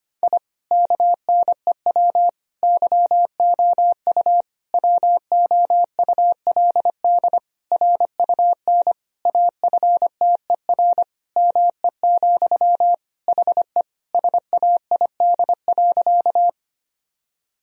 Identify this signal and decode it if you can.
Morse code, heard as a keyed tone, about 25 wpm, 700 Hz. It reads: I KNEW YOU WOULD RUN AFTER ME, HE SAID.
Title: Sentences from Top 1000 Words - Review Part 1 - 25wpm